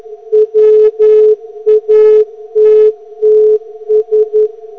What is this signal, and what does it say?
Morse code, heard as a keyed tone, about 11 wpm, 430 Hz. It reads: WATTS